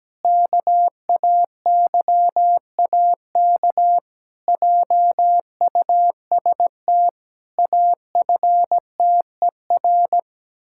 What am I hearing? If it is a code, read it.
Morse code, heard as a keyed tone, about 17 wpm, 700 Hz. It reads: KAYAK JUST AFTER
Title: Pangram Challenge v2 - 17wpm